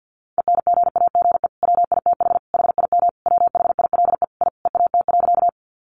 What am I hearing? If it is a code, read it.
Morse code, heard as a keyed tone, about 50 wpm, 700 Hz. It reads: ENGAGE WITH HIM WHILE I EAT.
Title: Sentences from Top 1500 Words 50wpm